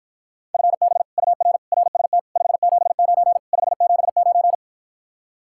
Morse code, heard as a keyed tone, about 54 wpm, 700 Hz. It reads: FB UR RST 579 579